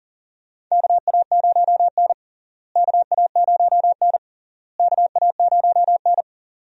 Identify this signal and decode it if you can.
Morse code, heard as a keyed tone, about 40 wpm, 700 Hz. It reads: KA0N KA0N KA0N
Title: USA Call Signs - 4 Characters - Repeated 3x - 40wpm